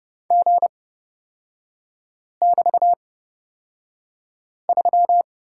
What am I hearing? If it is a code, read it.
Morse code, heard as a keyed tone, about 30 wpm, spaced as for 5 wpm, 700 Hz. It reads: G<BT>3
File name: Lesson 36e - BT - Three Characters 30wpm